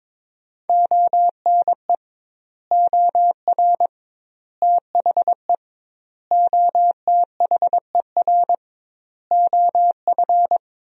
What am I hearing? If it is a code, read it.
Morse code, heard as a keyed tone, about 22 wpm, 700 Hz. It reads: ONE OR THE OTHER OF
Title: 2x Word-Spacing - Sentences from Top 400 Words - 22wpm